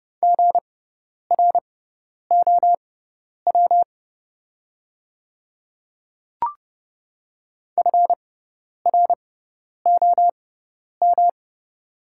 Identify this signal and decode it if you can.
Morse code, heard as a keyed tone, about 30 wpm, spaced as for 10 wpm, 700 Hz. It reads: GROW FROM
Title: 6x Character Spacing - Top 300 Words - 30wpm